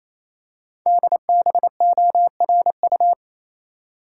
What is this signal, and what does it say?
Morse code, heard as a keyed tone, about 29 wpm, 700 Hz. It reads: DBORU
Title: Code Groups - 5 Letters - 28wpm